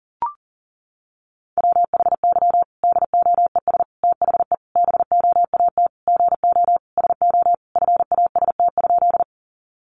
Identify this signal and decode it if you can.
Morse code, heard as a keyed tone, about 40 wpm, 700 Hz. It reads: WHY DOES THE BOAT GO SO FAST?
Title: Sentences from Top 500 Words 40wpm